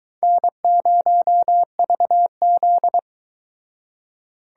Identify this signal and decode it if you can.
Morse code, heard as a keyed tone, about 22 wpm, 700 Hz. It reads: N0VZ